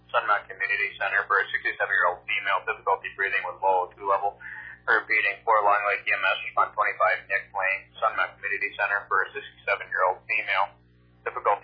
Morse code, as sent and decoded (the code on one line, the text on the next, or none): none